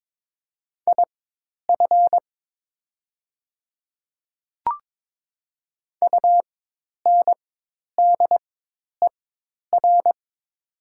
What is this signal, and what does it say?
Morse code, heard as a keyed tone, about 22 wpm, spaced as for 10 wpm, 700 Hz. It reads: IF UNDER
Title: 4x Character Spacing - Top 500 Words - 22wpm